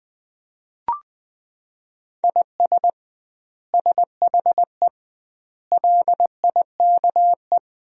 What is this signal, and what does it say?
Morse code, heard as a keyed tone, about 20 wpm, 700 Hz. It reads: IS SHE LIKE
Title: 2x Word-Spacing - Sentences from Top 100 Words - 20wpm